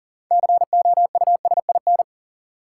Morse code, heard as a keyed tone, about 40 wpm, 700 Hz. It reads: COUSIN
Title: Six Letter Words - 40wpm